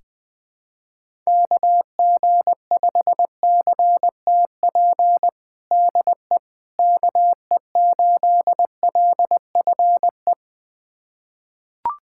Morse code, heard as a keyed tone, about 20 wpm, 700 Hz. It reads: KG5CTP DE KE8LFE
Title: USA Callsign DE Callsign - 6 Characters - 20wpm